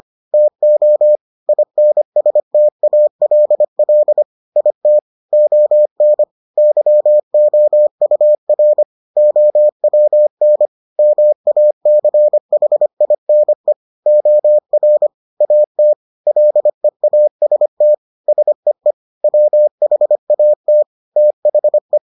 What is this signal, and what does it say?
Morse code, heard as a keyed tone, about 25 wpm, 600 Hz. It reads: TO INSTALL IT ON YOUR OWN MACHINE OR AT LEAST SEE WHAT THE